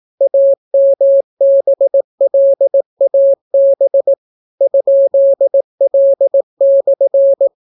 Morse code, heard as a keyed tone, about 18 wpm, 550 Hz. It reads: AMBLAB ?L/